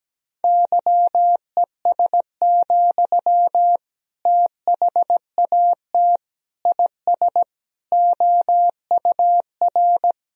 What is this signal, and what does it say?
Morse code, heard as a keyed tone, about 17 wpm, 700 Hz. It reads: YES, THAT IS OUR
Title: Sentences from Top 1000 Words - Review Part 3 - 17wpm